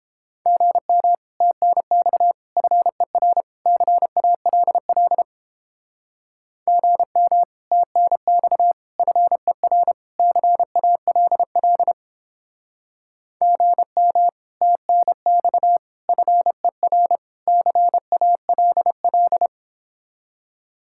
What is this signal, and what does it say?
Morse code, heard as a keyed tone, about 29 wpm, 700 Hz. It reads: GM TNX FER CALL GM TNX FER CALL GM TNX FER CALL